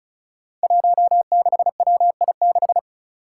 Morse code, heard as a keyed tone, about 34 wpm, 700 Hz. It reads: 16WI6
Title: Five Challenging Letter-Number Combinations 35wpm